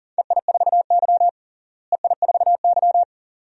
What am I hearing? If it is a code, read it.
Morse code, heard as a keyed tone, about 41 wpm, 700 Hz. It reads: EI4Y EI4Y
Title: DXCC-100 Call Signs - 4 Characters - Repeated 2x - 40wpm